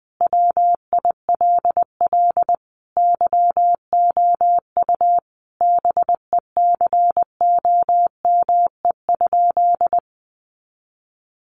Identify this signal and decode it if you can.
Morse code, heard as a keyed tone, about 19 wpm, 700 Hz. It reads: WILL YOU BECOME?
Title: Sentences from Top 600 Words 20wpm